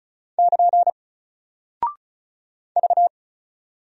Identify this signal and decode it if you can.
Morse code, heard as a keyed tone, about 36 wpm, 700 Hz. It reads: <KN> V
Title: Single Character (Letter:Number:Punctuation:Prosign) Alt 35wpm